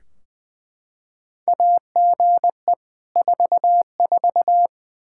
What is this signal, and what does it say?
Morse code, heard as a keyed tone, about 20 wpm, 700 Hz. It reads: AGE 44
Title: Practice QSO - Exchange 3 - More Info - 20wpm